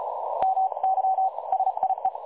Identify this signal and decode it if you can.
Morse code, heard as a keyed tone, about 33 wpm, 760 Hz. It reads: MY SIE
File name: dlazaawansowanych / nagraniazpasma / websdr recording 3565.0kHz